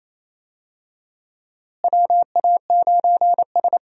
Morse code, heard as a keyed tone, about 28 wpm, 700 Hz. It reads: WA9S
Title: USA Call Signs - 4 Characters - Repeated 3x - 28wpm